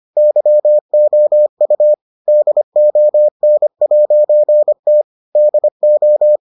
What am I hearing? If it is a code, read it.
Morse code, heard as a keyed tone, about 25 wpm, 600 Hz. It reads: YOU DON'T DO